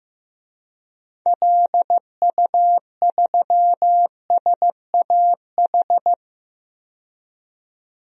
Morse code, heard as a keyed tone, about 15 wpm, 700 Hz. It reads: LU3SAH
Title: DXCC-100 Call Signs - 6 Characters - 15wpm